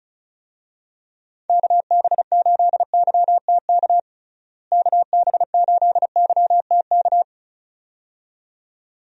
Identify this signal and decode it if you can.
Morse code, heard as a keyed tone, about 36 wpm, 700 Hz. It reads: KB8YTK KB8YTK
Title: USA Call Signs - 6 Characters - Repeated 2x - 35wpm